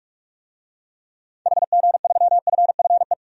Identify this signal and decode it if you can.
Morse code, heard as a keyed tone, about 46 wpm, 700 Hz. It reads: HG3FFE
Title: DXCC-100 Call Signs - 6 Characters - 45wpm